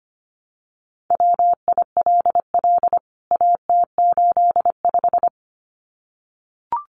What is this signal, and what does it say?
Morse code, heard as a keyed tone, about 25 wpm, 700 Hz. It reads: WILL AT85
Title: CWT Exchange 25wpm